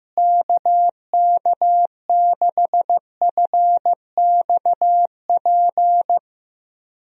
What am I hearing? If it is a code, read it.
Morse code, heard as a keyed tone, about 15 wpm, 700 Hz. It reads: KK6FXP
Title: USA Call Signs - 6 Characters - Repeated 2x - 15wpm